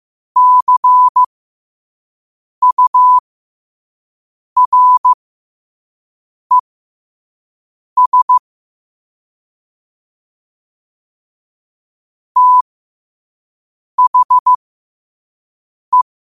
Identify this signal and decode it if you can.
Morse code, heard as a keyed tone, about 15 wpm, spaced as for 5 wpm, 1000 Hz. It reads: CURES THE
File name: qotdfiles / 5wpm